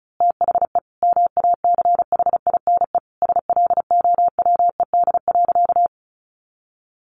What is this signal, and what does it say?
Morse code, heard as a keyed tone, about 34 wpm, 700 Hz. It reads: THE MACHINE SLOWED.
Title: A Sound Of Thunder 35wpm